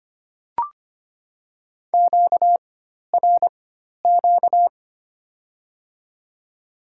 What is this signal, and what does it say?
Morse code, heard as a keyed tone, about 25 wpm, spaced as for 12 wpm, 700 Hz. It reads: QRQ
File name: Top 90 QSO Elements - 4x Character Spacing - 25wpm